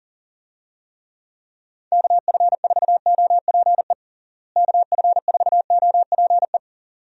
Morse code, heard as a keyed tone, about 37 wpm, 700 Hz. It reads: KF4OPE KF4OPE
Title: USA Call Signs - 6 Characters - Repeated 2x - 40wpm